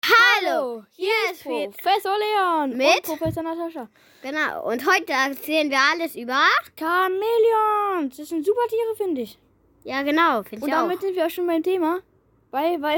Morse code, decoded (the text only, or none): none